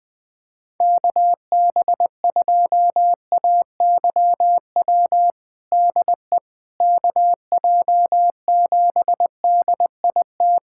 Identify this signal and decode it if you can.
Morse code, heard as a keyed tone, about 20 wpm, 700 Hz. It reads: KB2AYW DE KJ7DIT
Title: USA Callsign DE Callsign - 6 Characters - 20wpm